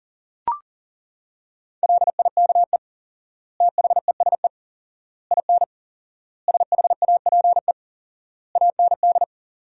Morse code, heard as a keyed tone, about 40 wpm, 700 Hz. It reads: LIKE THESE IN SHAPE AND
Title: 4x Word-Spacing - Sentences from Top 500 Words - 40wpm